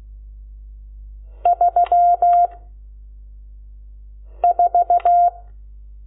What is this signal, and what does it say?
Morse code, heard as a keyed tone, about 16 wpm, 700 Hz. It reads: 3 4
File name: Lecciones / CLASE Nº12